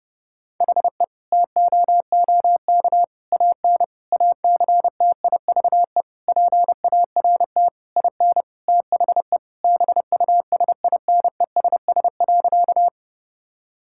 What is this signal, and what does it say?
Morse code, heard as a keyed tone, about 30 wpm, 700 Hz. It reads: HE TOOK AN ACTIVE PART IN THE BUSINESS.
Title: Sentences from Top 1000 Words - Review Part 6 - 30wpm